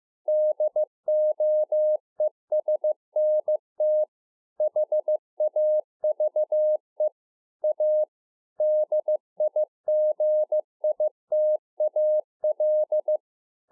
Morse code, heard as a keyed tone, about 15 wpm, 610 Hz. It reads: DOESNT HAVE A DIGITAL
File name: 2011-04-mp3 / A SMALL TRANCEIVER 2 15WPM